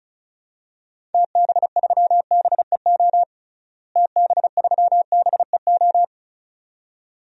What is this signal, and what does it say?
Morse code, heard as a keyed tone, about 34 wpm, 700 Hz. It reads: TB3BEO TB3BEO
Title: DXCC-100 Call Signs - 6 Characters - Repeated 2x - 35wpm